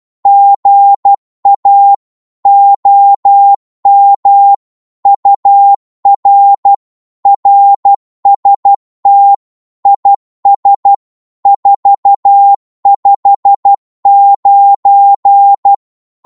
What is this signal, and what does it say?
Morse code, heard as a keyed tone, about 12 wpm, 800 Hz. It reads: GA OM UR RST IS 459
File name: Morse / qso-12wpm